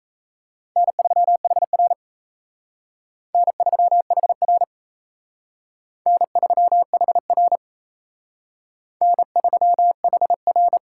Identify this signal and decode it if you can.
Morse code, heard as a keyed tone, about 35 wpm, 700 Hz. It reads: N3HR N3HR N3HR N3HR
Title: Speed Racing - USA 1x2 Call Signs 28wpm